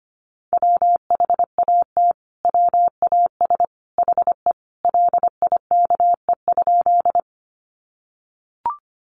Morse code, heard as a keyed tone, about 25 wpm, 700 Hz. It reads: WHAT WAS HE LIKE?
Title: Sentences from Top 100 Words 25wpm